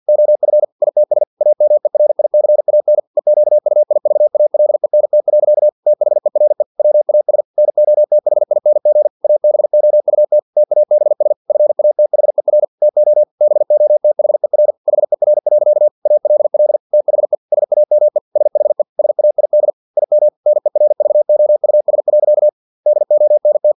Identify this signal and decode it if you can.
Morse code, heard as a keyed tone, about 50 wpm, 600 Hz. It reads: OF ITS AMERICAN EQUIVALENT. THERE WAS NOTHING ABOUT TABS FATHER TO BOTHER HER. ALL THE SAME SHE SAID IM NERVOUS. DONT
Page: Never 49wpm